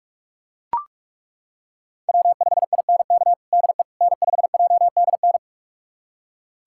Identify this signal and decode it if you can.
Morse code, heard as a keyed tone, about 42 wpm, 700 Hz. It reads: W5INK DE N5JDN
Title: USA Callsign DE Callsign - 5 Characters - 45wpm